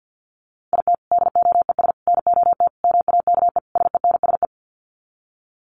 Morse code, heard as a keyed tone, about 50 wpm, 700 Hz. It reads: IT DOES NOT MAKE SENSE
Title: Sets of 5 Words Encore2 50wpm